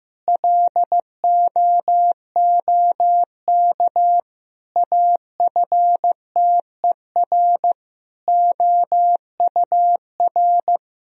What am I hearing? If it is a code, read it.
Morse code, heard as a keyed tone, about 15 wpm, 700 Hz. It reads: LOOK AFTER OUR
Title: Sentences from Top 1000 Words - Review Part 4 - 15wpm